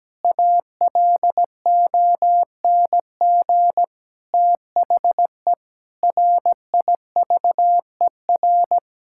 Morse code, heard as a keyed tone, about 17 wpm, 700 Hz. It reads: ALONG THE RIVER
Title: Sentences from Top 1000 Words - Review Part 6 - 17wpm